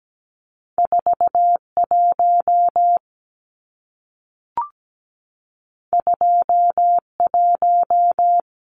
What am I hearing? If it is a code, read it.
Morse code, heard as a keyed tone, about 17 wpm, 700 Hz. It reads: 41 21
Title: Numbers - 2-Digit 17wpm